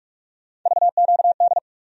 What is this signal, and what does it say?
Morse code, heard as a keyed tone, about 46 wpm, 700 Hz. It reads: VQD
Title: USA Call Sign Postfixes - 3 Letters 45wpm